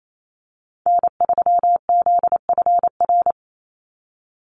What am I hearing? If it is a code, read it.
Morse code, heard as a keyed tone, about 28 wpm, 700 Hz. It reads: N3ZFR
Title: USA Call Signs 28wpm